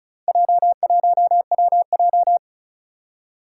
Morse code, heard as a keyed tone, about 36 wpm, 700 Hz. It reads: J1WJ
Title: Four Challenging Letter-Number Combinations 35wpm